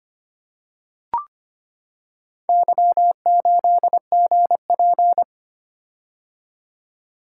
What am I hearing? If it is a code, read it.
Morse code, heard as a keyed tone, about 25 wpm, 690 Hz. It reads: Y8GP
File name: practice / 25wpm / Four Letter-Number Combinations - 25wpm